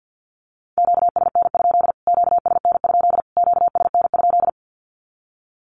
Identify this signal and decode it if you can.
Morse code, heard as a keyed tone, about 50 wpm, 700 Hz. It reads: QSN? QSN? QSN?
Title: Q-Codes Repeated 3x 50wpm